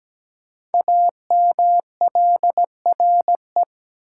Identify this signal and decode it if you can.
Morse code, heard as a keyed tone, about 17 wpm, 700 Hz. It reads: AMLRE
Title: Code Groups - 5 Letters - 17wpm